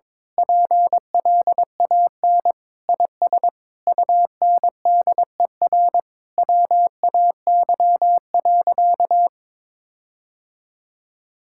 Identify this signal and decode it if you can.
Morse code, heard as a keyed tone, about 22 wpm, 700 Hz. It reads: PLAN IS UNDER WAY.
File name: Sentences with 9-Word Phrases - 22wpm